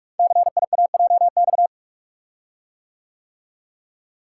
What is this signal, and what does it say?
Morse code, heard as a keyed tone, about 44 wpm, 700 Hz. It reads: CIAJX
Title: Code Groups - 5 Letters - 45wpm